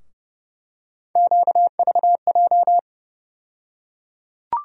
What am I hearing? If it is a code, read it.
Morse code, heard as a keyed tone, about 30 wpm, 700 Hz. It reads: QVJ
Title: USA Call Sign Postfixes - 3 Letters 30wpm